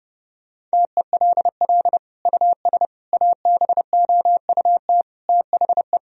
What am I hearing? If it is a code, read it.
Morse code, heard as a keyed tone, about 30 wpm, 700 Hz. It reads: TELL US ABOUT THE
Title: Sentences from Top 1000 Words - Review Part 3 - 30wpm